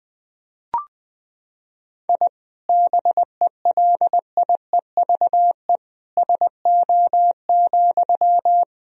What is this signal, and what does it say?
Morse code, heard as a keyed tone, about 21 wpm, 700 Hz. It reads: I BELIEVE SO,
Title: Sentences from Top 1000 Words - Review Part 4 - 20wpm